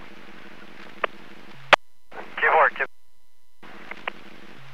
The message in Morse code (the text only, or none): none